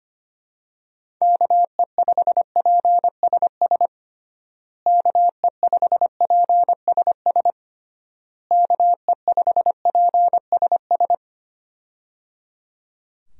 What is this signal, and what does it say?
Morse code, heard as a keyed tone, about 25 wpm, 700 Hz. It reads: KE5PSS KE5PSS KE5PSS
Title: USA Call Signs - 6 Characters - Repeated 3x - 25wpm